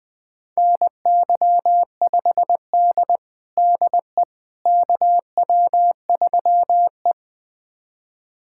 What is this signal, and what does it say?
Morse code, heard as a keyed tone, about 20 wpm, 700 Hz. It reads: NY5D DE KW3E
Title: USA Callsign DE Callsign - 4 Characters - 20wpm